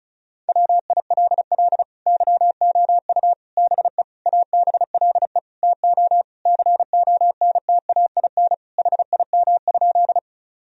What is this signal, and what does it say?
Morse code, heard as a keyed tone, about 36 wpm, 700 Hz. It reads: WILL YOU BE ABLE TO CONTAIN HIM?
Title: Sentences from Top 1000 Words - Review Part 2 - 35wpm